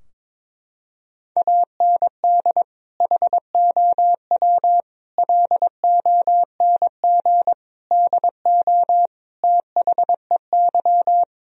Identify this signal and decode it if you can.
Morse code, heard as a keyed tone, about 22 wpm, 700 Hz. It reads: AND HOW LONG DO THEY
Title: Sentences from Top 1000 Words - Review Part 4 - 22wpm